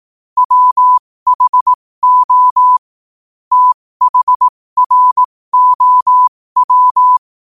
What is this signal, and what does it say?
Morse code, heard as a keyed tone, about 18 wpm, spaced as for 15 wpm, 1000 Hz. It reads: WHO THROW
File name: qotdfiles / 15wpm